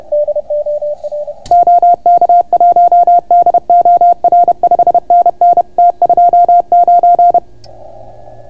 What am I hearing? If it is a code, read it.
Morse code, heard as a keyed tone, about 30 wpm, 690 Hz. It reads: OK1DOR 5NN T29